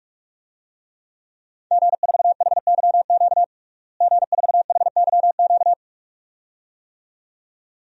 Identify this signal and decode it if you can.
Morse code, heard as a keyed tone, about 42 wpm, 700 Hz. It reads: G4HYQ G4HYQ
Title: DXCC-100 Call Signs - 5 Characters - Repeated 2x - 45wpm